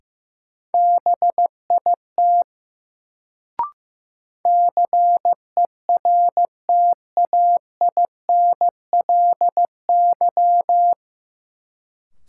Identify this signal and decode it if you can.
Morse code, heard as a keyed tone, about 15 wpm, 700 Hz. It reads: BIT CERTAINLY
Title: Sentences from Top 900 Words - New Word Drill - 15wpm